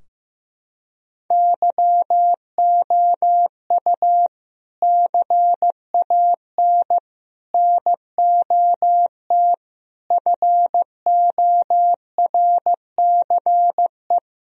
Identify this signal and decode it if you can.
Morse code, heard as a keyed tone, about 15 wpm, 700 Hz. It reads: YOU CAN NOT FORCE
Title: Sentences from Top 1000 Words - Review Part 3 - 15wpm